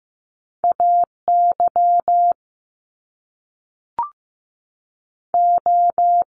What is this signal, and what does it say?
Morse code, heard as a keyed tone, about 15 wpm, 700 Hz. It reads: A Y O
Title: Top Two Letter Combinations in Words 15wpm